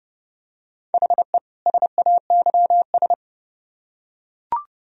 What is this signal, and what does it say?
Morse code, heard as a keyed tone, about 27 wpm, 700 Hz. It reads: HE SAYS